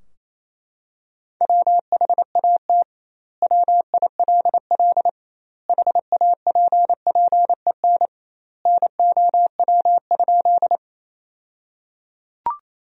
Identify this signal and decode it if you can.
Morse code, heard as a keyed tone, about 28 wpm, 700 Hz. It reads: WHAT WILL HAPPEN NOW?
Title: 2x Word-Spacing - Sentences from Top 400 Words - 28wpm